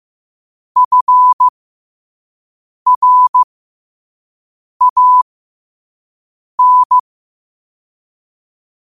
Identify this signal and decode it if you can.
Morse code, heard as a keyed tone, about 15 wpm, spaced as for 5 wpm, 1000 Hz. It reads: FRAN
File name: qotdfiles / 5wpm